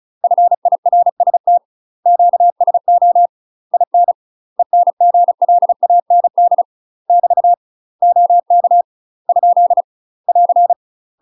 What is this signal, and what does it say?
Morse code, heard as a keyed tone, about 34 wpm, 700 Hz. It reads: FIRST QSO IN ENGLAND <BT> OK ? <AR>
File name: UK MASTER QSOs TRAINING / 35WPM / UK MASTER QSOs 0003